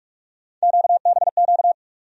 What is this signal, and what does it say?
Morse code, heard as a keyed tone, about 47 wpm, 700 Hz. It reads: QBQ
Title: USA Call Sign Postfixes - 3 Letters 45wpm